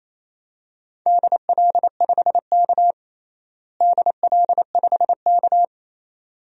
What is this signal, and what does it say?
Morse code, heard as a keyed tone, about 28 wpm, 700 Hz. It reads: DL5K DL5K
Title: DXCC-100 Call Signs - 4 Characters - Repeated 2x - 28wpm